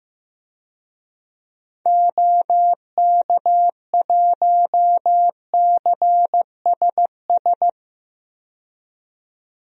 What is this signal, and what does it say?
Morse code, heard as a keyed tone, about 15 wpm, 700 Hz. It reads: OK1CSS
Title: DXCC-100 Call Signs - 6 Characters - 15wpm